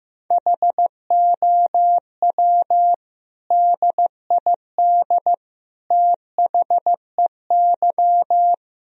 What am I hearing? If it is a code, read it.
Morse code, heard as a keyed tone, about 16 wpm, 700 Hz. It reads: HOW DID THEY
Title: Sentences from Top 1000 Words - Review Part 1 - 15wpm